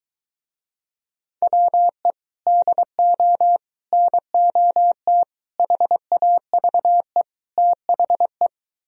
Morse code, heard as a keyed tone, about 22 wpm, 700 Hz. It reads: WE DO NOT HAVE THE